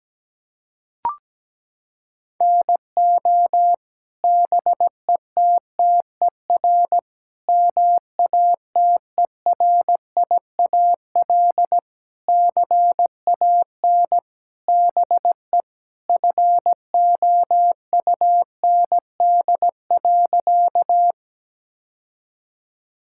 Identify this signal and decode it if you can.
Morse code, heard as a keyed tone, about 17 wpm, 700 Hz. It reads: NO BETTER MATERIAL CAN BE FOUND.
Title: Sentences from Top 1000 Words - Review Part 5 - 17wpm